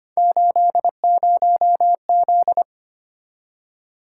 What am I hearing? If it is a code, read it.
Morse code, heard as a keyed tone, about 25 wpm, 700 Hz. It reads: 80Z
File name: Three Letter-Number Combinations - 25wpm